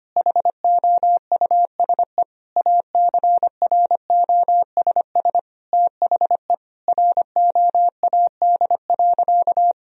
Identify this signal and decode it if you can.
Morse code, heard as a keyed tone, about 25 wpm, 700 Hz. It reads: HOUSE ACROSS THE ROAD.
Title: Sentences from Top 1000 Words - Review Part 1 - 25wpm